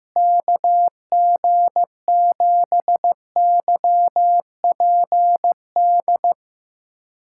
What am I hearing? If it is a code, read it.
Morse code, heard as a keyed tone, about 15 wpm, 700 Hz. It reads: KG7YPD
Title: USA Callsign DE Callsign 15wpm